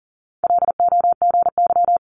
Morse code, heard as a keyed tone, about 39 wpm, 700 Hz. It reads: LOGY
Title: Multi-letter Phonemes 40wpm